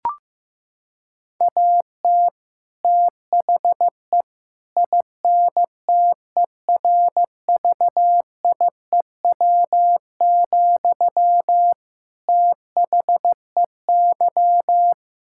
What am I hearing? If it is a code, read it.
Morse code, heard as a keyed tone, about 15 wpm, 700 Hz. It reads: AT THE INTERVIEW, THEY
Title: Sentences from Top 1600 Words 15wpm